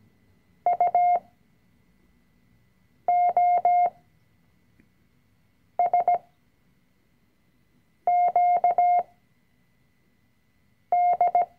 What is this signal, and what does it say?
Morse code, heard as a keyed tone, about 17 wpm, spaced as for 4 wpm, 710 Hz. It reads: UOSQD
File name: CW-Online-Kurs / Anfänger / Lektion 005